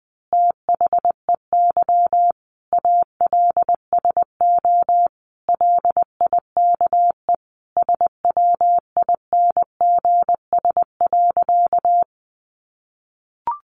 Morse code, heard as a keyed tone, about 20 wpm, 700 Hz. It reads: THEY ALSO LIKE SWINGS.
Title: Easy Phrases 20wpm